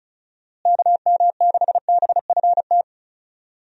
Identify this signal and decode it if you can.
Morse code, heard as a keyed tone, about 34 wpm, 700 Hz. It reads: KM6BFT
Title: USA Call Signs - 6 Characters - Repeated 3x - 35wpm